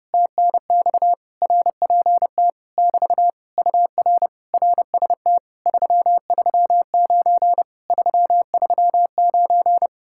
Morse code, heard as a keyed tone, about 31 wpm, 700 Hz. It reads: TNX RPT <BT> UR RST 339 339